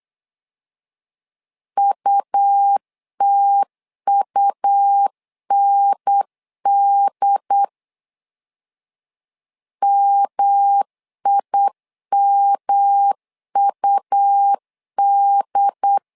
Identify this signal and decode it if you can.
Morse code, heard as a keyed tone, about 8 wpm, 800 Hz. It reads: UTUND MIMUD
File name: LEZIONI / 3 MWRO / 3 M 8wpm 2300